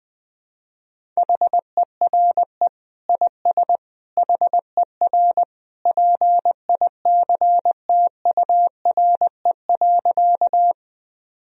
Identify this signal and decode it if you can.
Morse code, heard as a keyed tone, about 20 wpm, 700 Hz. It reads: HERE IS HER PICTURE.
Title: Sentences from Top 1000 Words - Review Part 1 - 20wpm